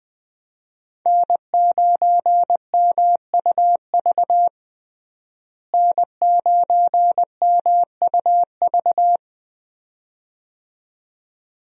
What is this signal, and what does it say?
Morse code, heard as a keyed tone, about 20 wpm, 700 Hz. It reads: N9MUV N9MUV
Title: USA Call Signs - 5 Characters - Repeated 2x - 20wpm